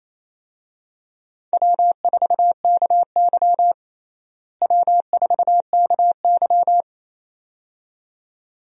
Morse code, heard as a keyed tone, about 28 wpm, 700 Hz. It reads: W4KY W4KY
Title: USA Call Signs - 4 Characters - Repeated 2x - 28wpm